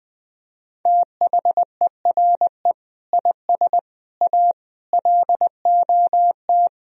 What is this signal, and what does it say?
Morse code, heard as a keyed tone, about 20 wpm, 700 Hz. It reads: THERE IS A LOT
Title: Sentences from Top 1000 Words - Review Part 4 - 20wpm